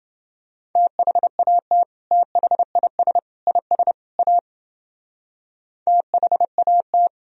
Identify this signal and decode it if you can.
Morse code, heard as a keyed tone, about 30 wpm, 700 Hz. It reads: THAT THIS IS A THAT